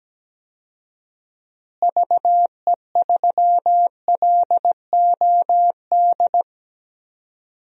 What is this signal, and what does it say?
Morse code, heard as a keyed tone, about 17 wpm, 700 Hz. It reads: VE3LOD
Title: DXCC-100 Call Signs - 6 Characters - Repeated 2x - 17wpm